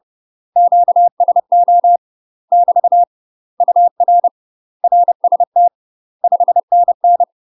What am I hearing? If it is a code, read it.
Morse code, heard as a keyed tone, about 30 wpm, 700 Hz. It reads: QSO <BT> UR RST 5NN
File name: UK MASTER QSOs TRAINING / 30WPM / UK MASTER QSOs 0038